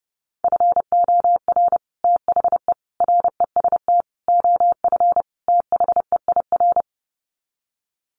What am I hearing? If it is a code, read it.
Morse code, heard as a keyed tone, about 28 wpm, 700 Hz. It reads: FOR THE REST OF THEIR
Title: Sets of 5 Words Encore2 30wpm